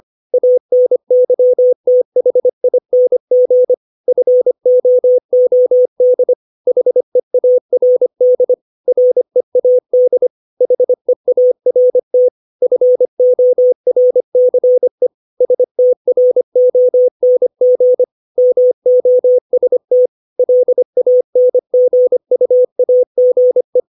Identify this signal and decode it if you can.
Morse code, heard as a keyed tone, about 25 wpm, 500 Hz. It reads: ANYTHING FOOD HEARD READ HEART FORCE STRONG MOST LANGUAGE
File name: 632 words / 632 Words 25WPM